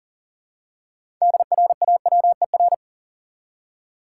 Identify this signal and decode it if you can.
Morse code, heard as a keyed tone, about 40 wpm, 700 Hz. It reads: DRAWER
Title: Six Letter Words - 40wpm